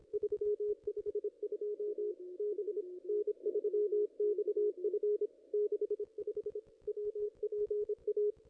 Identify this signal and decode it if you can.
Morse code, heard as a keyed tone, about 29 wpm, 410 Hz, with radio noise. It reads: 352 B N3XF 65 WPA
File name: contests / ss-cw-k5tr-op-n5rz -0100